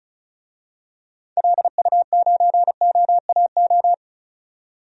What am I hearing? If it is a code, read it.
Morse code, heard as a keyed tone, about 36 wpm, 700 Hz. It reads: LU9OAO
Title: DXCC-100 Call Signs - 6 Characters - 35wpm